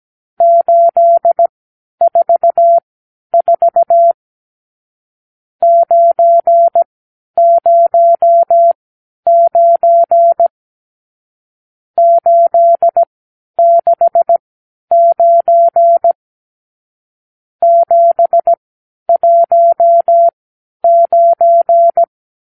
Morse code, heard as a keyed tone, about 17 wpm, spaced as for 10 wpm, 680 Hz. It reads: 844 909 869 719
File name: arrl / Review (1-0)